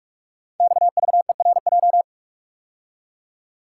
Morse code, heard as a keyed tone, about 44 wpm, 690 Hz. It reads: XVERJ
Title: Code Groups - 5 Letters - 45wpm